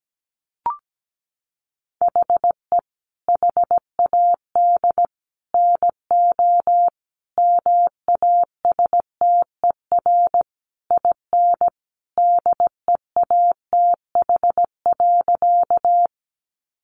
Morse code, heard as a keyed tone, about 17 wpm, 700 Hz. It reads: HE HAD NO MASTER IN DEATH.
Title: Sentences from Top 1300 Words 17wpm